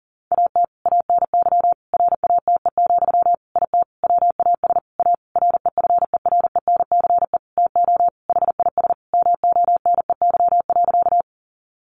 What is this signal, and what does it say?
Morse code, heard as a keyed tone, about 40 wpm, 700 Hz. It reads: AT ANY RATE, IT WAS A REFERENCE TO HIS MONEY.
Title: Sentences from Top 1200 Words 40wpm